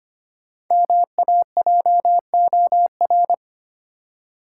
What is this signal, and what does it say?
Morse code, heard as a keyed tone, about 25 wpm, 700 Hz. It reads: MAJOR
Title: Five Letter Words - 25wpm